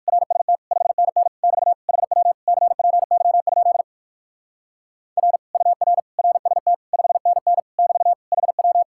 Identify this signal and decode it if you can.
Morse code, heard as a keyed tone, about 62 wpm, 700 Hz. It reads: RST 5NN <BT> HW CPY? R UR RST 5NN <BT> HW